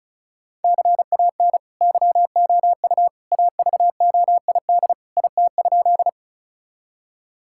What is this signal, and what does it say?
Morse code, heard as a keyed tone, about 36 wpm, 700 Hz. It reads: CAN YOU AVOID IT?
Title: Sentences from Top 1000 Words - Review Part 6 - 35wpm